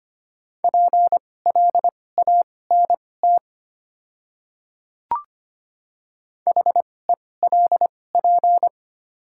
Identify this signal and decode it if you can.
Morse code, heard as a keyed tone, about 25 wpm, spaced as for 18 wpm, 700 Hz. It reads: PLANT HELP
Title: 2x Character Spacing - Top 500 Words - 25wpm